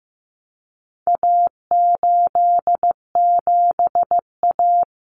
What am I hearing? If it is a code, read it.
Morse code, heard as a keyed tone, about 15 wpm, 700 Hz. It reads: A87A
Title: Common Cut Numbers - 4-Digit 15wpm